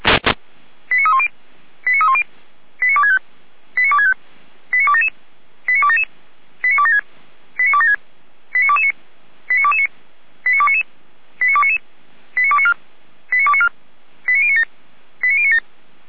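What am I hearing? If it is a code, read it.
Five-tone selective call (ZVEI), heard as quick runs of five tones, tones of about 70 ms. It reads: 89310 89310 89265 89265 89277 89277 89276 89276 89229 89229 89299 89299 89294 89294 89097 89097
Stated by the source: ZVEI five-tone selective call received on SDR.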